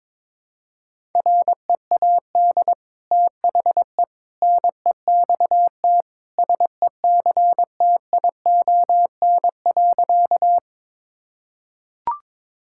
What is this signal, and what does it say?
Morse code, heard as a keyed tone, about 22 wpm, 700 Hz. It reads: READ THE NEXT SECTION.
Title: Sentences from Top 1000 Words - Review Part 5 - 22wpm